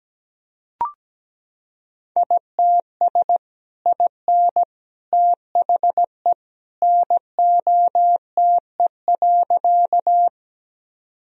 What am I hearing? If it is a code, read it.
Morse code, heard as a keyed tone, about 17 wpm, 700 Hz. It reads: ITS IN THE NOTE.
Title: Sentences from Top 1000 Words - Review Part 2 - 17wpm